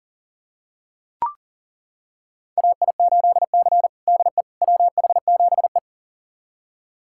Morse code, heard as a keyed tone, about 40 wpm, 700 Hz. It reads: AI8C DE WH7E
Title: USA Callsign DE Callsign - 4 Characters - 40wpm